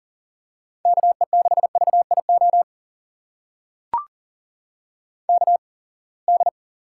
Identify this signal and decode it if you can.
Morse code, heard as a keyed tone, about 40 wpm, spaced as for 28 wpm, 700 Hz. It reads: KE6VIO KD
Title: USA Call Signs - 6 Characters - 8x Character Spacing - 40wpm